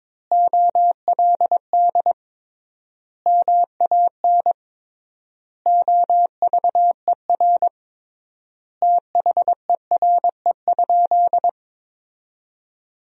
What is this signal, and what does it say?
Morse code, heard as a keyed tone, about 21 wpm, 700 Hz. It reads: OLD MAN OVER THERE?
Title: 3x Word-Spacing - Sentences from Top 200 Words - 22wpm